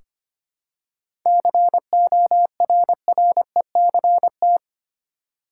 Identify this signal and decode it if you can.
Morse code, heard as a keyed tone, about 25 wpm, 700 Hz. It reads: CORRECT
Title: Seven Letter Words - 25wpm